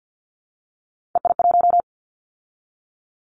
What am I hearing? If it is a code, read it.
Morse code, heard as a keyed tone, about 50 wpm, 700 Hz. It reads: EI1
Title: Callsign Prefixes for Top 50 DXCC - 50wpm